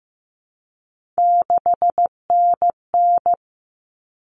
Morse code, heard as a keyed tone, about 15 wpm, 700 Hz. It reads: 6NN
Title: Common Cut Numbers - 3-Digit 15wpm